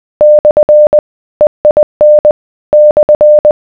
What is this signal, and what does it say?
Morse code, heard as a keyed tone, about 20 wpm, 600 Hz. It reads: / EIN /